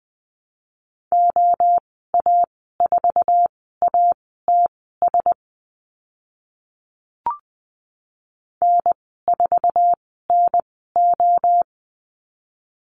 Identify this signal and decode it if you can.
Morse code, heard as a keyed tone, about 20 wpm, spaced as for 14 wpm, 700 Hz. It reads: OA4ATS N4NO
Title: 055 Lesson 10g - Letter r - Call Signs 20wpm